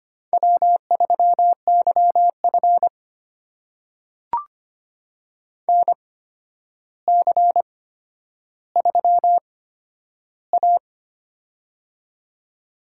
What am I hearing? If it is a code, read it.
Morse code, heard as a keyed tone, about 25 wpm, spaced as for 11 wpm, 700 Hz. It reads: W3YF NC3A
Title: USA Call Signs - 4 Characters - 8x Character Spacing - 25wpm